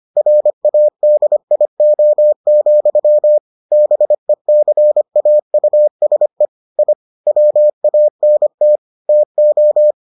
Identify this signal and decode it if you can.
Morse code, heard as a keyed tone, about 25 wpm, 600 Hz. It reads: RADIO, BECAUSE I WANT TO